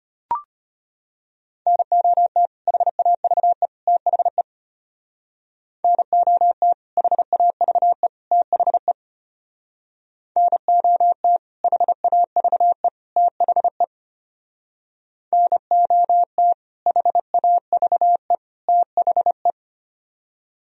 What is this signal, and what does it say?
Morse code, heard as a keyed tone, about 32 wpm, 700 Hz. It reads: NOT HAVE THE NOT HAVE THE NOT HAVE THE NOT HAVE THE